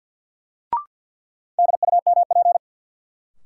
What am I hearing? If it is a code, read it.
Morse code, heard as a keyed tone, about 52 wpm, 700 Hz. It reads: DUMP
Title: Sentences from Top 1300 Words - New Word Drill - 50wpm